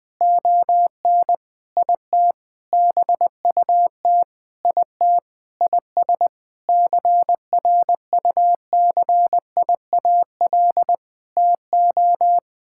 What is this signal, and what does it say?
Morse code, heard as a keyed tone, about 20 wpm, 700 Hz. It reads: ON IT BUT IT IS CRUCIAL TO
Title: Sentences with 9-Word Phrases - 20wpm